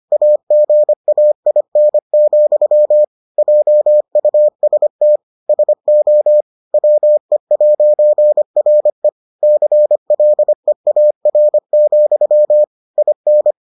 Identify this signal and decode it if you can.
Morse code, heard as a keyed tone, about 25 wpm, 600 Hz. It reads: AGAIN, JUST SO WE'RE CLEAR, IN